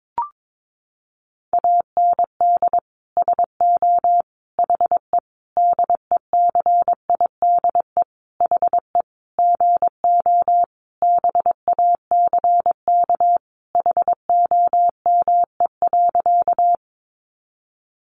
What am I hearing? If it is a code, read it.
Morse code, heard as a keyed tone, about 22 wpm, 700 Hz. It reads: AND SO HE DECIDE HE GO BACK HOME.
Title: Sentences from Top 500 Words 22wpm